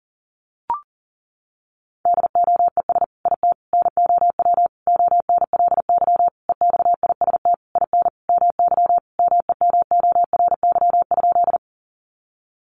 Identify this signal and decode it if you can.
Morse code, heard as a keyed tone, about 40 wpm, 700 Hz. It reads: DOES IT NOW ONLY EXIST IN MY MEMORY?
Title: Sentences from Top 1300 Words 40wpm